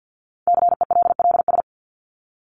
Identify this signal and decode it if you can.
Morse code, heard as a keyed tone, about 50 wpm, 700 Hz. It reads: CELLS
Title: ES:S Suffix - Top 100 Words with Base Words 50wpm